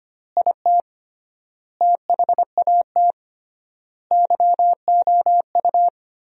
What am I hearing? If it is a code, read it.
Morse code, heard as a keyed tone, about 25 wpm, 700 Hz. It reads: IT THAT YOU
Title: 3x Word-Spacing - Sentences from Top 300 Words - 25wpm